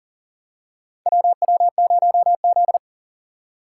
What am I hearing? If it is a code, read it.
Morse code, heard as a keyed tone, about 38 wpm, 700 Hz. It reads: WW0Z
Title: USA Call Signs - 4 Characters - 3x Character Spacing - 40wpm